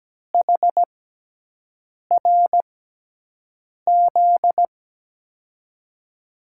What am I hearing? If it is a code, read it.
Morse code, heard as a keyed tone, about 17 wpm, spaced as for 6 wpm, 700 Hz. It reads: HRZ
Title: USA Call Signs - 5 Characters - 6x Character Spacing - 17wpm